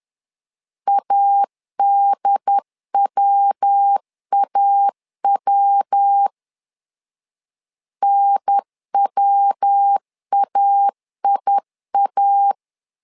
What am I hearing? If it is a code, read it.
Morse code, heard as a keyed tone, about 10 wpm, 800 Hz. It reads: ADWAW NWAIA
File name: LEZIONI / 3 MWRO / 3 MW 10wpm 1959